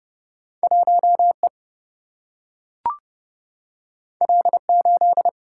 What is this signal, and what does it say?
Morse code, heard as a keyed tone, about 30 wpm, 700 Hz. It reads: 1 E L 8
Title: Single-Letter Single-Number 30wpm